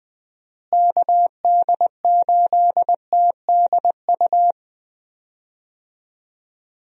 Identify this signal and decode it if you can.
Morse code, heard as a keyed tone, about 20 wpm, 700 Hz. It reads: KD8TDU
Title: USA Call Signs - 6 Characters - Repeated 3x - 20wpm